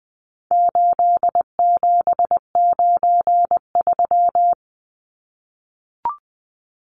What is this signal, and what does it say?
Morse code, heard as a keyed tone, about 20 wpm, 700 Hz. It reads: 8793
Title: Numbers - 4-Digit 20wpm